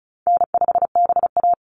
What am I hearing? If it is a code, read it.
Morse code, heard as a keyed tone, about 34 wpm, 700 Hz. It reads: N5BA
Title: Easy Call Signs Repeated 3x - 35wpm